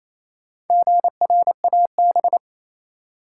Morse code, heard as a keyed tone, about 28 wpm, 700 Hz. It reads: GRAB
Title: Four Letter Words - 28wpm